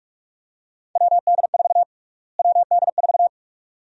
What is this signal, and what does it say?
Morse code, heard as a keyed tone, about 46 wpm, 700 Hz. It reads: WD4 WD4
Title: USA Call Sign Prefixes - Repeated 2x - 45wpm